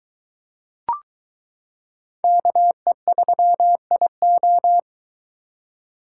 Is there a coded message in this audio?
Yes: Morse code from 2.2 to 4.8 s, heard as a keyed tone, about 23 wpm, 700 Hz. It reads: KE3IO